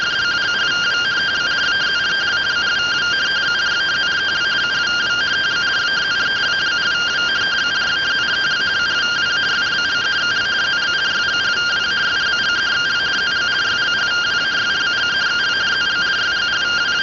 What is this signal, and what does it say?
Two-tone teleprinter signal, about 170 Hz shift.